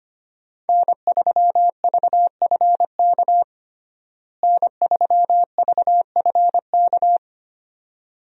Morse code, heard as a keyed tone, about 25 wpm, 700 Hz. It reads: N3VFK N3VFK
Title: USA Call Signs - 5 Characters - Repeated 2x - 25wpm